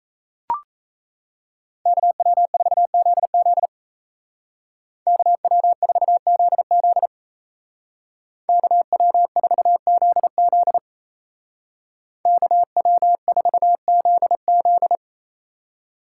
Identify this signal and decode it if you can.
Morse code, heard as a keyed tone, about 34 wpm, 700 Hz. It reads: KW4ZZ KW4ZZ KW4ZZ KW4ZZ
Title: Speed Racing - USA 2x2 Call Signs 28wpm